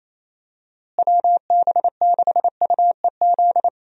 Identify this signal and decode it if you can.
Morse code, heard as a keyed tone, about 28 wpm, 700 Hz. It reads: WB6UEZ
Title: USA Call Signs - 6 Characters - Repeated 2x - 28wpm